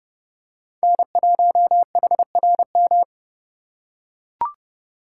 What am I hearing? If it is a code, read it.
Morse code, heard as a keyed tone, about 28 wpm, 700 Hz. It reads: N1HRM